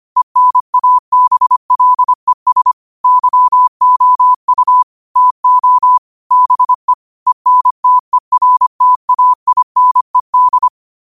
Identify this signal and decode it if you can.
Morse code, heard as a keyed tone, about 25 wpm, 1000 Hz. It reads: ENABLES YOU TO BE ENTERTAINED
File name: qotdfiles / 25wpm